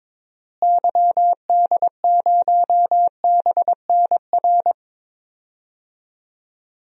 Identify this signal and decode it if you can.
Morse code, heard as a keyed tone, about 22 wpm, 700 Hz. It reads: YD0BNR
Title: DXCC-100 Call Signs - 6 Characters - Repeated 3x - 22wpm